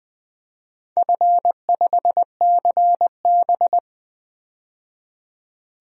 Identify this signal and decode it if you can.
Morse code, heard as a keyed tone, about 20 wpm, 700 Hz. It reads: F5CB
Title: DXCC-100 Call Signs - 4 Characters - 20wpm